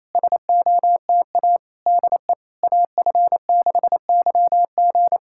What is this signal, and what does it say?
Morse code, heard as a keyed tone, about 28 wpm, 690 Hz. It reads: SOTA DE AF6YG
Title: SOTA QSO Practice 28wpm